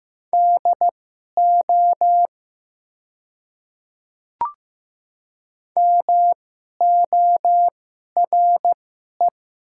Morse code, heard as a keyed tone, about 15 wpm, spaced as for 11 wpm, 700 Hz. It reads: DO MORE
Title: 2x Character Spacing - Top 100 Words - 15wpm